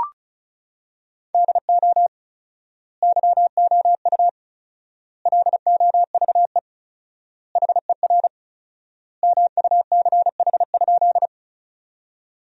Morse code, heard as a keyed tone, about 36 wpm, 700 Hz. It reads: DO YOU LOVE HER MUCH?
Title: 4x Word-Spacing - Sentences from Top 400 Words - 35wpm